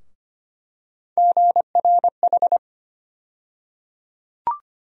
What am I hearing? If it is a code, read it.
Morse code, heard as a keyed tone, about 25 wpm, 700 Hz. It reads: GRH
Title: USA Call Sign Postfixes - 3 Letters 25wpm